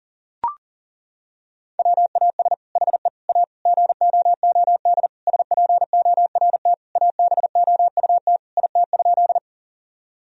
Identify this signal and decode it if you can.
Morse code, heard as a keyed tone, about 37 wpm, 700 Hz. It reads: WAS HE A GOOD SPORT ABOUT IT?
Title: Sentences from Top 1000 Words - Review Part 4 - 40wpm